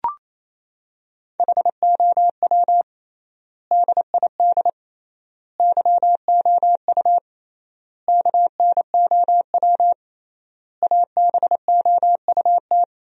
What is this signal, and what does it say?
Morse code, heard as a keyed tone, about 28 wpm, 700 Hz. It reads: HOW DID YOU KNOW ABOUT
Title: 3x Word-Spacing - Sentences from Top 100 Words - 28wpm